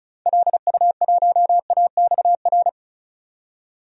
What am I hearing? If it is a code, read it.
Morse code, heard as a keyed tone, about 36 wpm, 700 Hz. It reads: LU1AXR
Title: DXCC-100 Call Signs - 6 Characters - 35wpm